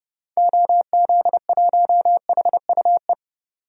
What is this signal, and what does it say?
Morse code, heard as a keyed tone, about 30 wpm, 700 Hz. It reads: OZ1HUE
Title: DXCC-100 Call Signs - 6 Characters - 30wpm